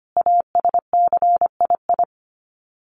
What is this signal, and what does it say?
Morse code, heard as a keyed tone, about 25 wpm, 700 Hz. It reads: ASCII
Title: Sets of 5 Words 25wpm